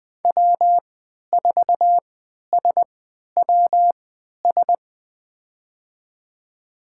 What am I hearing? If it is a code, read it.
Morse code, heard as a keyed tone, about 20 wpm, spaced as for 11 wpm, 700 Hz. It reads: W4SWS
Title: USA Call Signs - 5 Characters - 3x Character Spacing - 20wpm